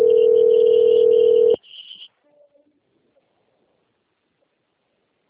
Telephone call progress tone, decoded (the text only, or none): ringback tone